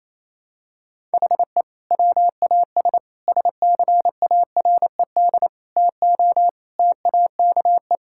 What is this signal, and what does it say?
Morse code, heard as a keyed tone, about 28 wpm, 700 Hz. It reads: HE WAS SCARED TO TAKE
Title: Sentences from Top 1000 Words - Review Part 6 - 28wpm